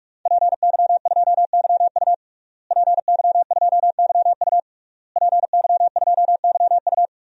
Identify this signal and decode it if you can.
Morse code, heard as a keyed tone, about 41 wpm, 700 Hz. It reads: PY2YU PY2YU PY2YU
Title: DXCC-100 Call Signs - 5 Characters - Repeated 3x - 45wpm